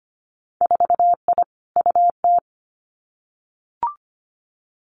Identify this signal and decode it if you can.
Morse code, heard as a keyed tone, about 25 wpm, 700 Hz. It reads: 4I UT
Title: Winter Field Day Exchanges Encore 25wpm